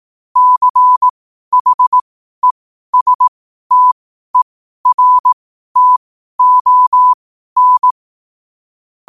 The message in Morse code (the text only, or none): CHESTERTON